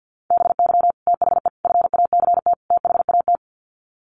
Morse code, heard as a keyed tone, about 48 wpm, 700 Hz. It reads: BY THE FACT THAT
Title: Sets of 4 Words 50wpm